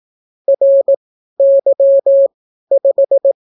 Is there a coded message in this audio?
Yes: Morse code from 0.5 to 3.3 s, heard as a keyed tone, about 18 wpm, spaced as for 12 wpm, 550 Hz. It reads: RY5